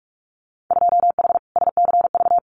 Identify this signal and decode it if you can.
Morse code, heard as a keyed tone, about 46 wpm, 700 Hz. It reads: 2H SCV
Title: Winter Field Day Exchanges Encore 45wpm